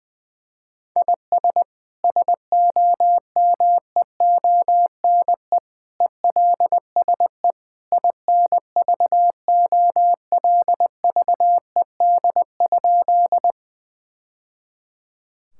Morse code, heard as a keyed tone, about 20 wpm, 700 Hz. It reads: IS SOMEONE ELSE INVOLVED?
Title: Sentences from Top 1000 Words - Review Part 5 - 20wpm